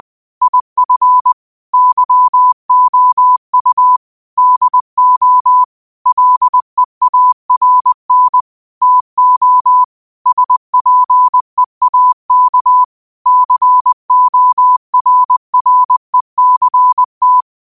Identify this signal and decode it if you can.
Morse code, heard as a keyed tone, about 20 wpm, 1000 Hz. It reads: IF YOU DO LEARN TO SPEAK CORRECT
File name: qotdfiles / 20wpm